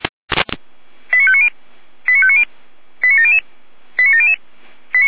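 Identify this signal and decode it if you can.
Five-tone selective call (ZVEI), heard as quick runs of five tones, tones of about 70 ms. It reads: 89590 89590 89700 89700
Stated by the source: ZVEI five-tone selective call, received on an SDR.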